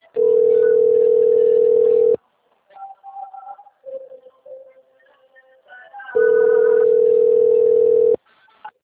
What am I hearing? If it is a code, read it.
Ringback tone (North America), 440 + 480 Hz.
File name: rg-1600-unknown- - - .54761